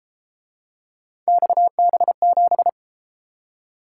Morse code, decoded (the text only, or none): XB7